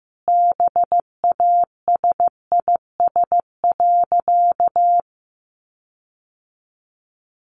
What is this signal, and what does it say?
Morse code, heard as a keyed tone, about 14 wpm, 700 Hz. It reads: BASIS.
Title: Sentences from Top 1500 Words 15wpm